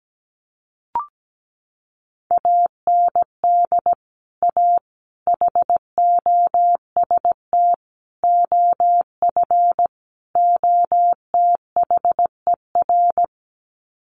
Morse code, heard as a keyed tone, about 17 wpm, 700 Hz. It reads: AND A HOST OF OTHER
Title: Sets of 5 Words Encore2 17wpm